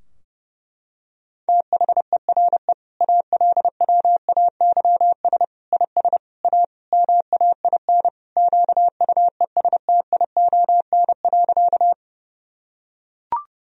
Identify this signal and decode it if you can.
Morse code, heard as a keyed tone, about 30 wpm, 700 Hz. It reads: THERE ALWAYS IS A MAIN QUESTION.
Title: Sentences from Top 1000 Words - Review Part 2 - 30wpm